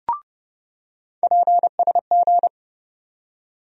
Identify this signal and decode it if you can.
Morse code, heard as a keyed tone, about 30 wpm, 700 Hz. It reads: PSG
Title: USA Call Sign Postfixes - 3 Letters 30wpm